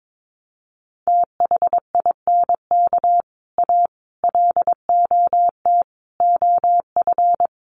Morse code, heard as a keyed tone, about 22 wpm, 700 Hz. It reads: THINK A LOT OF
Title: Sets of 5 Words 22wpm